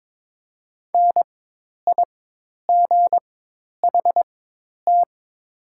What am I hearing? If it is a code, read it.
Morse code, heard as a keyed tone, about 22 wpm, spaced as for 10 wpm, 700 Hz. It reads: NIGHT
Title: 4x Character Spacing - Top 300 Words - 22wpm